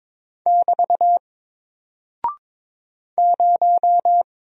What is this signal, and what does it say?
Morse code, heard as a keyed tone, about 22 wpm, 700 Hz. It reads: <BT> 0
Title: Single Character (Letter:Number:Punctuation:Prosign) Alt 22wpm